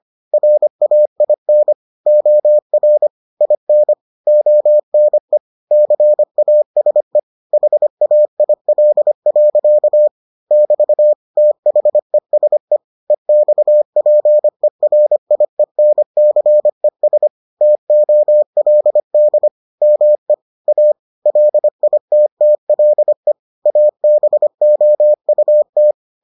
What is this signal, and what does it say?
Morse code, heard as a keyed tone, about 25 wpm, 600 Hz. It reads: RAIN OR IN ONE CASE HAIL. <BT> THESE EXPERIENCES TOLD ME A LITTLE ABOUT